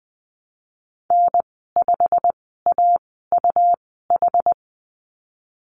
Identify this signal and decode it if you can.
Morse code, heard as a keyed tone, about 19 wpm, spaced as for 14 wpm, 700 Hz. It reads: N5AUH
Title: 154 Lesson 25g - Letter p - Call Signs 20wpm